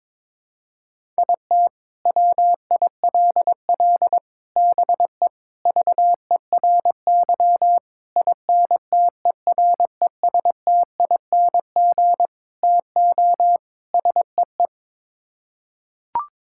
Morse code, heard as a keyed tone, about 22 wpm, 700 Hz. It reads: IT WILL BE VERY INTERESTING TO SEE
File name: practice / 22wpm / Sets of 7 Words - 22wpm